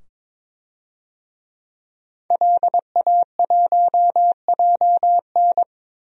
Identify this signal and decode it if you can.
Morse code, heard as a keyed tone, about 22 wpm, 700 Hz. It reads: LA1JN
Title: DXCC-100 Call Signs - 5 Characters - 22wpm